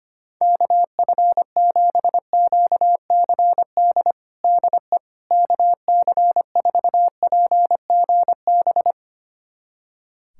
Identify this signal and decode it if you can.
Morse code, heard as a keyed tone, about 25 wpm, 700 Hz. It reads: KF7QCD DE KC4PGB
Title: USA Callsign DE Callsign 25wpm